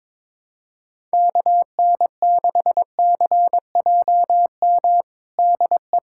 Morse code, heard as a keyed tone, about 22 wpm, 700 Hz. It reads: KN6CJM DE
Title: USA Callsign DE Callsign 22wpm